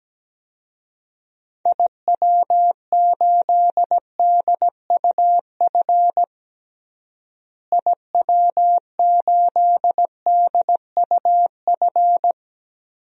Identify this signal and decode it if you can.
Morse code, heard as a keyed tone, about 17 wpm, 700 Hz. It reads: IW8DUF IW8DUF
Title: DXCC-100 Call Signs - 6 Characters - Repeated 2x - 17wpm